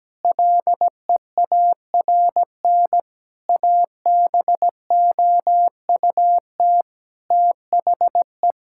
Morse code, heard as a keyed tone, about 17 wpm, 700 Hz. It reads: LEARN ABOUT THE
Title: Sentences from Top 1000 Words - Review Part 6 - 17wpm